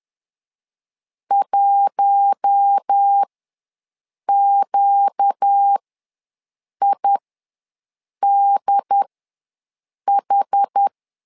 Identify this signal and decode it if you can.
Morse code, heard as a keyed tone, about 10 wpm, spaced as for 6 wpm, 800 Hz. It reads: 1QIDH
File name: LEZIONI / 4 Q1 ws 10wpm 2538